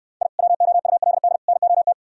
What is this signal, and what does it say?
Morse code, heard as a keyed tone, about 66 wpm, 700 Hz. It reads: I WOULD NOT